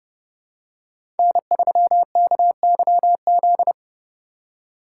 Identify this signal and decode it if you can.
Morse code, heard as a keyed tone, about 31 wpm, 700 Hz. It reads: N3KYZ
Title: USA Call Signs - 5 Characters - 2x Character Spacing - 30wpm